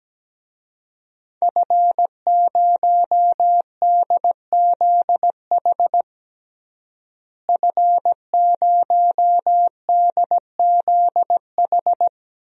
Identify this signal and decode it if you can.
Morse code, heard as a keyed tone, about 17 wpm, 700 Hz. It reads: F0DZH F0DZH
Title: DXCC-100 Call Signs - 5 Characters - Repeated 2x - 17wpm